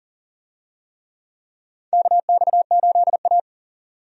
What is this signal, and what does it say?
Morse code, heard as a keyed tone, about 40 wpm, 700 Hz. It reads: KX8A
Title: USA Call Signs - 4 Characters - 40wpm